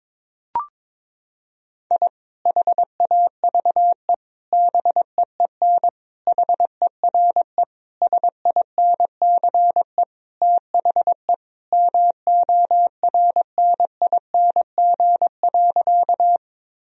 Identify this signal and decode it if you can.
Morse code, heard as a keyed tone, about 22 wpm, 700 Hz. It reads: I HAVE BEEN HERE SINCE THE MORNING.
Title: Sentences from Top 1000 Words - Review Part 1 - 22wpm